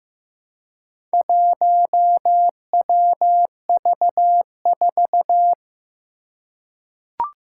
Morse code, heard as a keyed tone, about 15 wpm, 700 Hz. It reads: 1WV4
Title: Four Challenging Letter-Number Combinations 15wpm